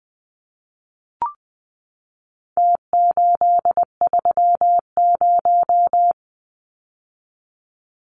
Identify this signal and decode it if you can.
Morse code, heard as a keyed tone, about 20 wpm, 700 Hz. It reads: T830
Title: Numbers - Contest Style - 4-Digit 20wpm